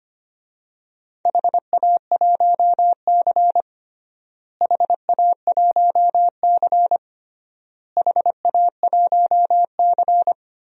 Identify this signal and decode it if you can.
Morse code, heard as a keyed tone, about 25 wpm, 700 Hz. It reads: HA1C HA1C HA1C
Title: DXCC-100 Call Signs - 4 Characters - Repeated 3x - 25wpm